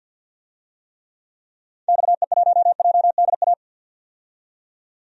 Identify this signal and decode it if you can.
Morse code, heard as a keyed tone, about 48 wpm, 700 Hz. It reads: XE1JDA
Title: DXCC-100 Call Signs - 6 Characters - 50wpm